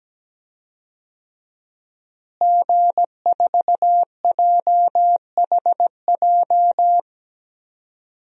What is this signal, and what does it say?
Morse code, heard as a keyed tone, about 17 wpm, 700 Hz. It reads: G4JHJ
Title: DXCC-100 Call Signs - 5 Characters - Repeated 3x - 17wpm